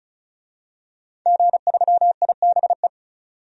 Morse code, heard as a keyed tone, about 34 wpm, 700 Hz. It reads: G3IBE
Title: DXCC-100 Call Signs - 5 Characters - 35wpm